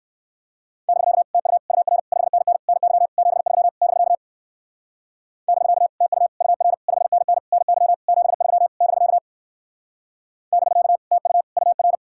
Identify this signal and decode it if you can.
Morse code, heard as a keyed tone, about 52 wpm, 700 Hz. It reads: <BK> TU UR 5NN NY 73 <BK> <BK> TU UR 5NN NY 73 <BK> <BK> TU UR
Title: Speed-Racing Common QSO Phrases Encore - 45wpm